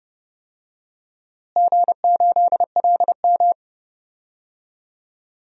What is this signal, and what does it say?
Morse code, heard as a keyed tone, about 29 wpm, 700 Hz. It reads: G8LM
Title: DXCC-100 Call Signs - 4 Characters - Repeated 3x - 30wpm